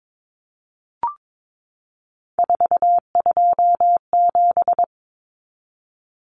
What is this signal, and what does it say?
Morse code, heard as a keyed tone, about 22 wpm, 700 Hz. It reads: 427
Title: Common Cut Numbers - 3-Digit 22wpm